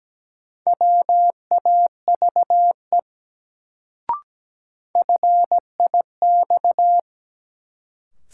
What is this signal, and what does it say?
Morse code, heard as a keyed tone, about 17 wpm, 700 Hz. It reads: WAVE FIX
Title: Sentences from Top 1400 Words - New Word Drill - 17wpm